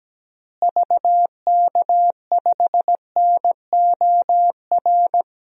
Morse code, heard as a keyed tone, about 17 wpm, 700 Hz. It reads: VK5NOR
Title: DXCC-100 Call Signs - 6 Characters - Repeated 3x - 17wpm